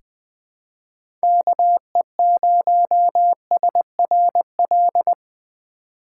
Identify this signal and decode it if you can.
Morse code, heard as a keyed tone, about 20 wpm, 700 Hz. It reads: KE0SRL
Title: K1USN SST QSO Practice - 20wpm